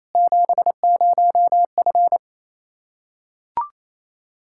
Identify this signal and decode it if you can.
Morse code, heard as a keyed tone, about 28 wpm, 700 Hz. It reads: 70F
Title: K1USN SST QSO Practice 28wpm